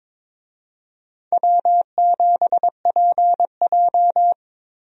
Morse code, heard as a keyed tone, about 23 wpm, 700 Hz. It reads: W7PJ
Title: USA Call Signs - 4 Characters - 22wpm